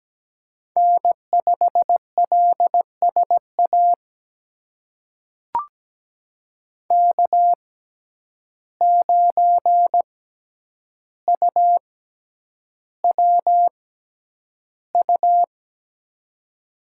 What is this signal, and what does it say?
Morse code, heard as a keyed tone, about 17 wpm, spaced as for 9 wpm, 700 Hz. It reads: N5LSA K9UWU
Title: USA Call Signs - 5 Characters - 6x Character Spacing - 17wpm